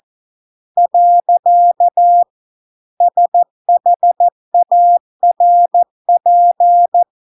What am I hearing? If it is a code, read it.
Morse code, heard as a keyed tone, about 14 wpm, 700 Hz. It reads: . SHARP